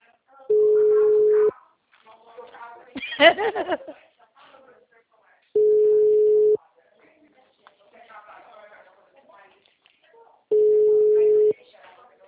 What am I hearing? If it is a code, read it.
Ringback tone (Europe, 425 Hz), with speech.